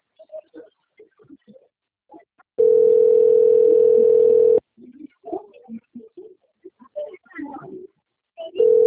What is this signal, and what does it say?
Ringback tone (North America), 440 + 480 Hz.